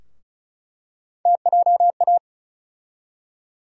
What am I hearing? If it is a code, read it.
Morse code, heard as a keyed tone, about 36 wpm, 700 Hz. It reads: TJA
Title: USA Call Sign Postfixes - 3 Letters 35wpm